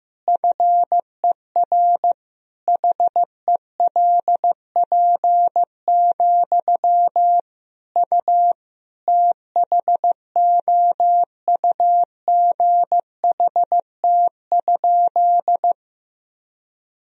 Morse code, heard as a keyed tone, about 16 wpm, 700 Hz. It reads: FER HELP, U THOUGHT?
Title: Sentences from Top 300 Words and Abbreviations 15wpm